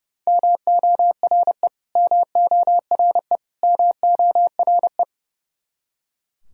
Morse code, heard as a keyed tone, about 30 wpm, 700 Hz. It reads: MORE MORE MORE
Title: Top 100 Words Repeated 30wpm